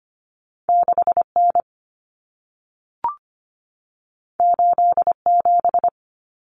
Morse code, heard as a keyed tone, about 25 wpm, 700 Hz. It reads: 6N 87
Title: Common Cut Numbers - 2-Digit 25wpm